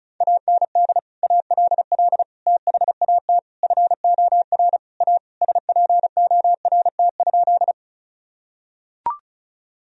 Morse code, heard as a keyed tone, about 36 wpm, 700 Hz. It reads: AND ALL THAT FOR A SPORT?
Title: Sentences from Top 1000 Words - Review Part 4 - 35wpm